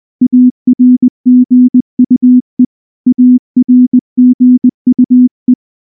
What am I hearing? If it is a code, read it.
Morse code, heard as a keyed tone, about 19 wpm, 250 Hz. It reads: ARGUE ARGUE
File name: morse / morse-training-20WPM-4